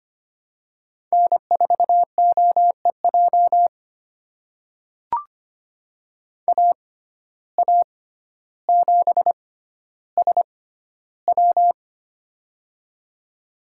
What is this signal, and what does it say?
Morse code, heard as a keyed tone, about 25 wpm, spaced as for 13 wpm, 700 Hz. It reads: N4OEJ AA7SW
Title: USA Call Signs - 5 Characters - 6x Character Spacing - 25wpm